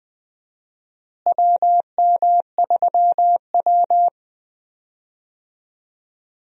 Morse code, heard as a keyed tone, about 20 wpm, 700 Hz. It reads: WM3W
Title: USA Call Signs - 4 Characters - Repeated 2x - 20wpm